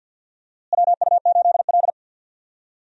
Morse code, heard as a keyed tone, about 53 wpm, 700 Hz. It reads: WU8L
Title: USA Call Signs - 4 Characters - Repeated 3x - 50wpm